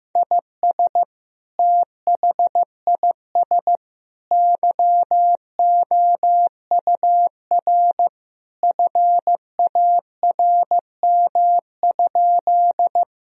Morse code, heard as a keyed tone, about 15 wpm, 700 Hz. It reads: IS THIS YOUR FARM?
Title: Sentences from Top 1000 Words - Review Part 2 - 15wpm